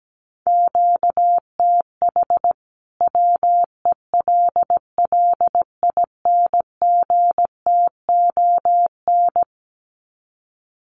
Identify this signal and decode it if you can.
Morse code, heard as a keyed tone, about 17 wpm, 700 Hz. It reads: QTH WELLINGTON
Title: Common QSO Phrases 17wpm